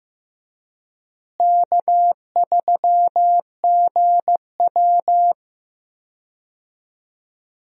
Morse code, heard as a keyed tone, about 15 wpm, 700 Hz. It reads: K3GW
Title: USA Call Signs - 4 Characters - Repeated 3x - 15wpm